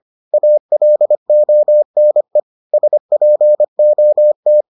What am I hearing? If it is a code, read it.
Morse code, heard as a keyed tone, about 25 wpm, 600 Hz. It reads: ALONE SPOT